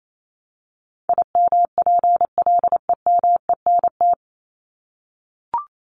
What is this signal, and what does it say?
Morse code, heard as a keyed tone, about 28 wpm, 700 Hz. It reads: IMPLEMENT
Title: Top 1501-1750 Words 28wpm